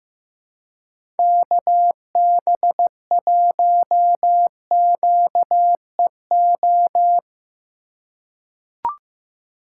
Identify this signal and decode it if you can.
Morse code, heard as a keyed tone, about 15 wpm, 700 Hz. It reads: KB1QEO
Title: USA Call Signs - 6 Characters - 4x Character Spacing - 15wpm